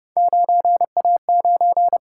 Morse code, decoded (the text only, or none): 9A9